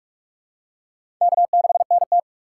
Morse code, heard as a keyed tone, about 46 wpm, 700 Hz. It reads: K6NT
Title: USA Call Signs - 4 Characters - 2x Character Spacing - 45wpm